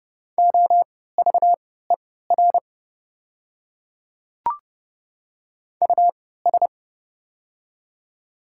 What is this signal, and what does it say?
Morse code, heard as a keyed tone, about 30 wpm, spaced as for 17 wpm, 690 Hz. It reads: OVER US
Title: 3x Character Spacing - Top 200 Words - 30wpm